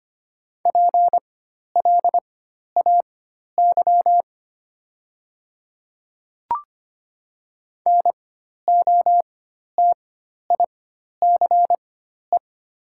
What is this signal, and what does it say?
Morse code, heard as a keyed tone, about 25 wpm, spaced as for 12 wpm, 700 Hz. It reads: PLAY NOTICE
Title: 4x Character Spacing - Top 500 Words - 25wpm